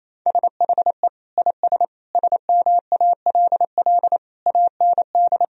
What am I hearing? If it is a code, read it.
Morse code, heard as a keyed tone, about 28 wpm, 700 Hz. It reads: SHE IS SMALL AND
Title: Sentences from Top 1000 Words - Review Part 1 - 28wpm